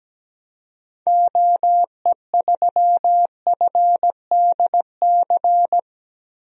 Morse code, heard as a keyed tone, about 17 wpm, 700 Hz. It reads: OE3FDC
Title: DXCC-100 Call Signs - 6 Characters - Repeated 2x - 17wpm